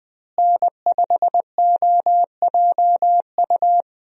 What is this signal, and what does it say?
Morse code, heard as a keyed tone, about 20 wpm, 700 Hz. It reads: N5OJU
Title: USA Call Signs - 5 Characters - Repeated 2x - 20wpm